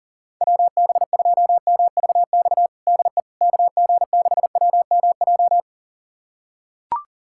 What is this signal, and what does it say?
Morse code, heard as a keyed tone, about 40 wpm, 700 Hz. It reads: WB2MVX DE KG6WMJ
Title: USA Callsign DE Callsign - 6 Characters - 40wpm